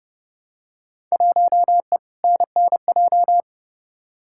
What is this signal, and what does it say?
Morse code, heard as a keyed tone, about 30 wpm, 700 Hz. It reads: 1E NNJ
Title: Field Day Exchanges - 30wpm